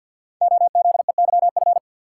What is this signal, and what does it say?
Morse code, heard as a keyed tone, about 47 wpm, 700 Hz. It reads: OZEYF
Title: Code Groups - 5 Letter:Numbers - 50wpm